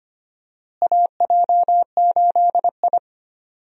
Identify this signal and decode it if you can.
Morse code, heard as a keyed tone, about 25 wpm, 700 Hz. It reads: AJ8I
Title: USA Call Signs - 4 Characters - Repeated 2x - 25wpm